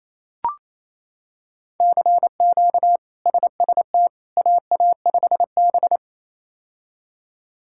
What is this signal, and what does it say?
Morse code, heard as a keyed tone, about 27 wpm, 700 Hz. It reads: CQ SST AA5B
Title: K1USN SST QSO Practice - 28wpm